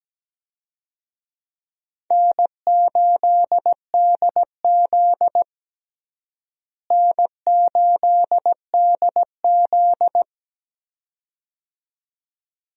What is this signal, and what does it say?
Morse code, heard as a keyed tone, about 17 wpm, 700 Hz. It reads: N8DZ N8DZ
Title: USA Call Signs - 4 Characters - Repeated 2x - 17wpm